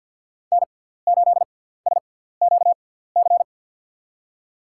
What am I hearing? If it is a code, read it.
Morse code, heard as a keyed tone, about 49 wpm, spaced as for 17 wpm, 690 Hz. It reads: N8SQC
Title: USA Call Signs - 5 Characters - 6x Character Spacing - 50wpm